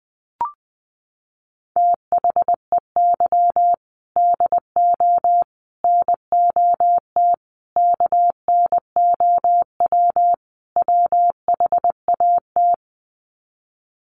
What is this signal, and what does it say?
Morse code, heard as a keyed tone, about 20 wpm, 700 Hz. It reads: THEY DO NOT KNOW WHAT
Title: Sets of 5 Words Encore 20wpm